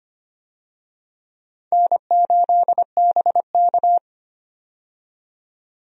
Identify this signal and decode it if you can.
Morse code, heard as a keyed tone, about 25 wpm, 700 Hz. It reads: N8BK
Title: USA Call Signs - 4 Characters - 25wpm